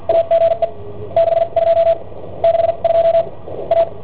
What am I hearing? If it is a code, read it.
Morse code, heard as a keyed tone, about 50 wpm, spaced as for 41 wpm, 670 Hz, with radio noise. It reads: AGE 62 62 A